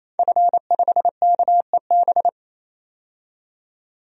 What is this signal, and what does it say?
Morse code, heard as a keyed tone, about 28 wpm, 700 Hz. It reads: F5KEB
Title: DXCC-100 Call Signs - 5 Characters - 28wpm